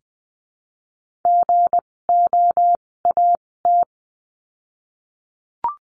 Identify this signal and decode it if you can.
Morse code, heard as a keyed tone, about 20 wpm, spaced as for 16 wpm, 700 Hz. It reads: GOAT
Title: 256 Lesson 42a - Words 12-20wpm